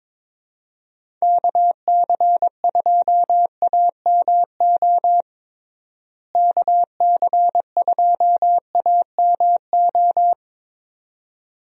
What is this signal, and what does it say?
Morse code, heard as a keyed tone, about 22 wpm, 700 Hz. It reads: KC2AMO KC2AMO
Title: USA Call Signs - 6 Characters - Repeated 2x - 22wpm